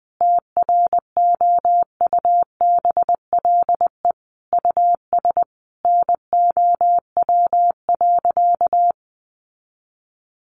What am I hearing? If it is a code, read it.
Morse code, heard as a keyed tone, about 19 wpm, 700 Hz. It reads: TROUBLE US NOW.
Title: Sentences from Top 1400 Words 20wpm